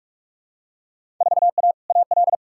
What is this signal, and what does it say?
Morse code, heard as a keyed tone, about 47 wpm, 700 Hz. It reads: 4A AL
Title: Field Day Exchanges - 45wpm